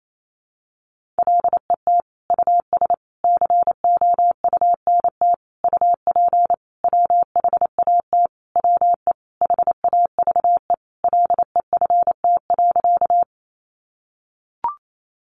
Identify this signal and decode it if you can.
Morse code, heard as a keyed tone, about 28 wpm, 700 Hz. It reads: LET US COUNT UP WHAT WE HAVE LEFT.
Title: Sentences from Top 1500 Words 28wpm